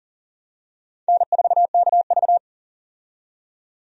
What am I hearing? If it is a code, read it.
Morse code, heard as a keyed tone, about 40 wpm, 700 Hz. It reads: N4KV
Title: USA Call Signs - 4 Characters - 40wpm